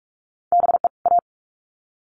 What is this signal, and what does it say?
Morse code, heard as a keyed tone, about 44 wpm, 700 Hz. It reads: BE A
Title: Sets of 2 Words 45wpm